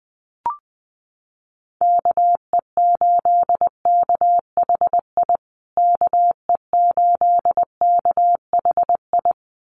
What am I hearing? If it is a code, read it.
Morse code, heard as a keyed tone, about 20 wpm, 700 Hz. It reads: KE8KHI KE8KHI
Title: USA Call Signs Repeated 2x 20wpm